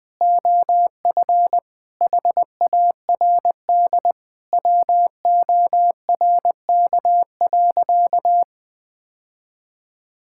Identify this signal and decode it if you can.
Morse code, heard as a keyed tone, about 20 wpm, 700 Hz. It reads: OF HARD WORK.
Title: Sentences with 3-Word Phrases - Spoken Beginning - 20wpm